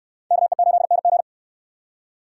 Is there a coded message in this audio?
Yes: Morse code from 0.3 to 1.2 s, heard as a keyed tone, about 67 wpm, 700 Hz. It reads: KE8NG